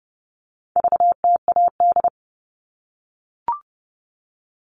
Morse code, heard as a keyed tone, about 30 wpm, 700 Hz. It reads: VTAD
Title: All Cut Numbers - 4-Digit 30wpm